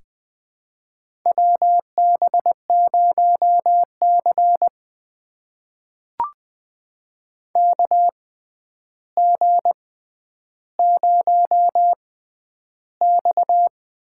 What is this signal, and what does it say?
Morse code, heard as a keyed tone, about 20 wpm, spaced as for 10 wpm, 700 Hz. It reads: WB0C KG0X
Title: USA Call Signs - 6x Character Spacing - 20wpm